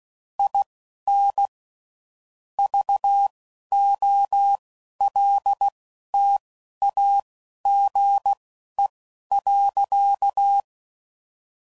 Morse code, heard as a keyed tone, about 16 wpm, spaced as for 11 wpm, 800 Hz. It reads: IN VOLTAGE.
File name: CW Code Groups @10WPM / Technical Subjects / XC